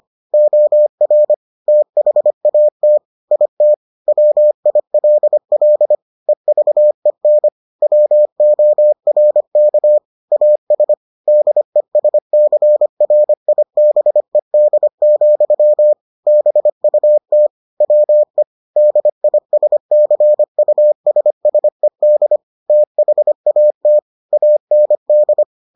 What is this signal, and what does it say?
Morse code, heard as a keyed tone, about 25 wpm, 600 Hz. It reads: OR THAT IT WILL EVEN WORK AS DESCRIBED, BUT WE DISCUSSED THAT AND